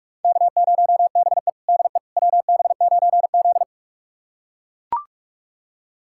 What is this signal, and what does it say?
Morse code, heard as a keyed tone, about 41 wpm, 700 Hz. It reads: K0BE DE WB9Z
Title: USA Callsign DE Callsign - 4 Characters - 45wpm